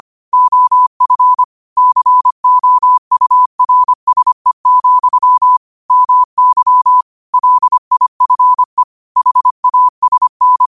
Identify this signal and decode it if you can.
Morse code, heard as a keyed tone, about 25 wpm, 1000 Hz. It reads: OF COURSE, MY LIFE HASN